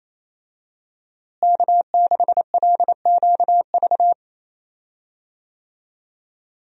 Morse code, heard as a keyed tone, about 28 wpm, 700 Hz. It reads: K6LQV
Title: USA Call Signs - 5 Characters - Repeated 2x - 28wpm